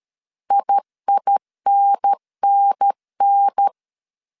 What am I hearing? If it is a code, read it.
Morse code, heard as a keyed tone, about 13 wpm, 800 Hz. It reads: IINNN
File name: CW / DET / 1 3 1500C S B 12wpm 2828